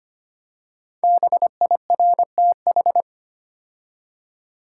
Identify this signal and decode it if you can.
Morse code, heard as a keyed tone, about 25 wpm, 700 Hz. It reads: BIRTH
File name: 5 Letter Words - First 3 Letters Given - 25wpm